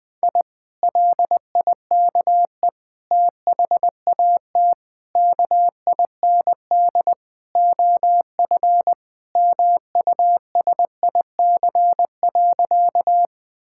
Morse code, heard as a keyed tone, about 21 wpm, 700 Hz. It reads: I LIKE THAT KIND OF MUSIC.
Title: Sentences with 3-Word Phrases - 20wpm